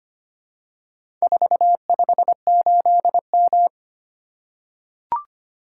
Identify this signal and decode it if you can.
Morse code, heard as a keyed tone, about 25 wpm, 700 Hz. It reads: 458M
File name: practice / 25wpm / Four Letter-Number Combinations - 25wpm